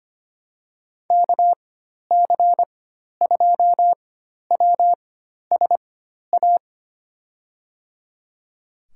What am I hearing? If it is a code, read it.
Morse code, heard as a keyed tone, about 25 wpm, spaced as for 12 wpm, 700 Hz. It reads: KC2WSA
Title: USA Call Signs - 6 Characters - 4x Character Spacing - 25wpm